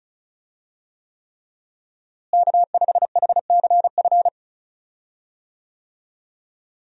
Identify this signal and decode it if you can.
Morse code, heard as a keyed tone, about 36 wpm, 700 Hz. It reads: K5HCF
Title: USA Call Signs - 5 Characters - Repeated 3x - 35wpm